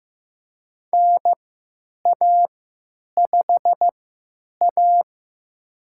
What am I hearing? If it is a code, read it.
Morse code, heard as a keyed tone, about 16 wpm, spaced as for 9 wpm, 700 Hz. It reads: NA5A
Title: USA Call Signs - 4 Characters - 3x Character Spacing - 15wpm